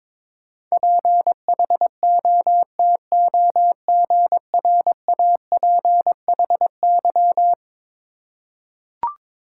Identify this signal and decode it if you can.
Morse code, heard as a keyed tone, about 21 wpm, 700 Hz. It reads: PHOTOGRAPHY
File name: practice / 22wpm / Eleven Letter Words - 22wpm